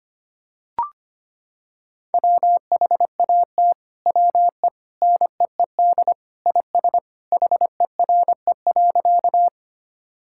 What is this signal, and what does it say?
Morse code, heard as a keyed tone, about 25 wpm, 700 Hz. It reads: WHAT WE NEED IS HERE.
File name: practice / 25wpm / Sentences with 2-Word Phrases - 25wpm